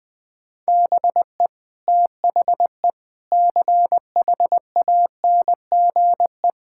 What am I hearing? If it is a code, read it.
Morse code, heard as a keyed tone, about 21 wpm, 700 Hz. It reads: BE THE CHANGE
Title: Sentences with 8-Word Phrases - Spoken Beginning - 20wpm